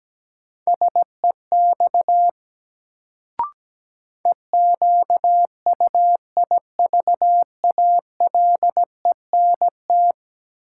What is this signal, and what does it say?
Morse code, heard as a keyed tone, about 17 wpm, 700 Hz. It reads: SEX EQUIVALENT
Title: Sentences from Top 1600 Words - New Word Drill Alt - 17wpm